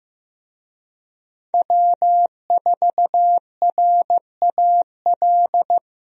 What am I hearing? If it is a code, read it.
Morse code, heard as a keyed tone, about 15 wpm, 700 Hz. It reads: W4RAL
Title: USA Call Signs - 5 Characters - Repeated 2x - 15wpm